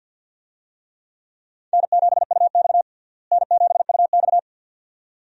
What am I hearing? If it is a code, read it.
Morse code, heard as a keyed tone, about 52 wpm, 700 Hz. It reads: N7UX N7UX
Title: USA Call Signs - 4 Characters - Repeated 2x - 50wpm